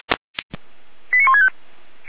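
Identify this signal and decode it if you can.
Five-tone selective call (ZVEI), heard as quick runs of five tones, tones of about 70 ms. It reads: 89165
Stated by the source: ZVEI five-tone call 89123 received by SDR, logged by the Orlamünde fire brigade (feuerwehr-orlamuende).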